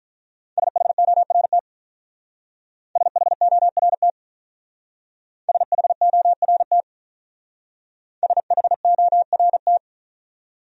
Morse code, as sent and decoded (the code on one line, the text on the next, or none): ... .... --- .-. - / ... .... --- .-. - / ... .... --- .-. - / ... .... --- .-. -
SHORT SHORT SHORT SHORT